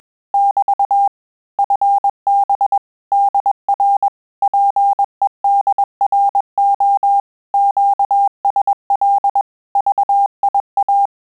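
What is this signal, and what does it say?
Morse code, heard as a keyed tone, about 21 wpm, 800 Hz. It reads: <BT> FB DR PEDRO QSL VIA
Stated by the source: Morse QSO practice at 20 words per minute, in Italian and English.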